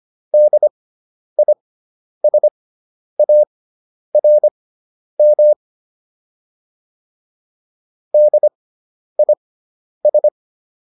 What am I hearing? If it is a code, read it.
Morse code, heard as a keyed tone, about 25 wpm, spaced as for 10 wpm, 600 Hz. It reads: DISARM DIS